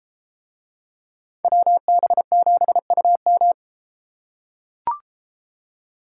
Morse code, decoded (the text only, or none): WB7UM